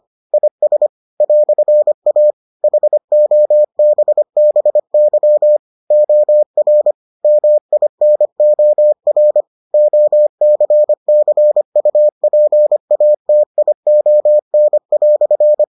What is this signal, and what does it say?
Morse code, heard as a keyed tone, about 25 wpm, 600 Hz. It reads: IS "A HOBBY OR MINOR OCCUPATION"